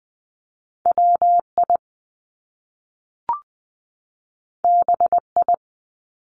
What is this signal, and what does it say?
Morse code, heard as a keyed tone, about 20 wpm, 700 Hz. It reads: W I B I
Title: Top Two Letter Combinations in Words 20wpm